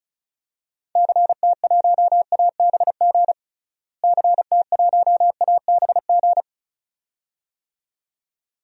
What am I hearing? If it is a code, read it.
Morse code, heard as a keyed tone, about 34 wpm, 700 Hz. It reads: CT1ABG CT1ABG
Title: DXCC-100 Call Signs - 6 Characters - Repeated 2x - 35wpm